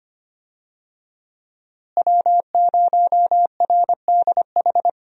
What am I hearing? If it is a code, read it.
Morse code, heard as a keyed tone, about 25 wpm, 700 Hz. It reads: W0RDH
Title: USA Call Signs - 5 Characters - Repeated 2x - 25wpm